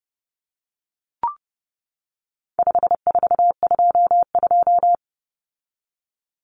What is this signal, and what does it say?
Morse code, heard as a keyed tone, about 30 wpm, 700 Hz. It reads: 5422
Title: Numbers - 4-Digit 30wpm